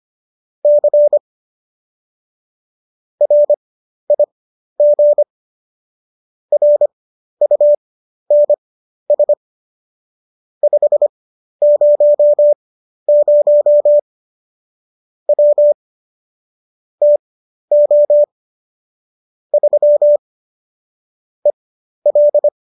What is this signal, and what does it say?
Morse code, heard as a keyed tone, about 25 wpm, spaced as for 12 wpm, 590 Hz. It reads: C RIG RUNS 500 W TO 3 EL
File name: qso012 12